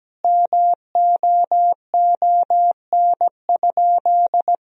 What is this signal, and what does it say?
Morse code, heard as a keyed tone, about 17 wpm, 700 Hz. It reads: MOON?
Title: Sentences from Top 1000 Words - Review Part 3 - 17wpm